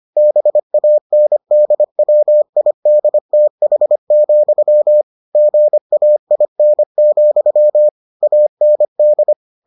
Morse code, heard as a keyed tone, about 25 wpm, 600 Hz. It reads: BANDWIDTH, GAIN, AND